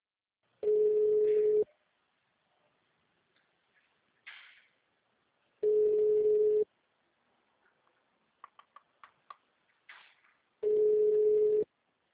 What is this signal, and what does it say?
Ringback tone (Europe, 425 Hz).